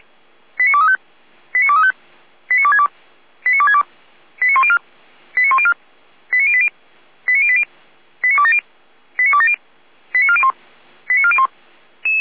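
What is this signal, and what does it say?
Five-tone selective call (ZVEI), heard as quick runs of five tones, tones of about 70 ms. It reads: 89235 89235 89262 89262 89104 89104 89080 89080 89270 89270 89401 89401
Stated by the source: ZVEI five-tone selective call received with an SDR.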